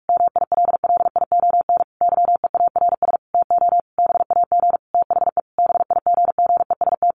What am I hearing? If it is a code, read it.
Morse code, heard as a keyed tone, about 44 wpm, 700 Hz. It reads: MILLION YEARS TO BAG THE BIGGEST
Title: A Sound Of Thunder 45wpm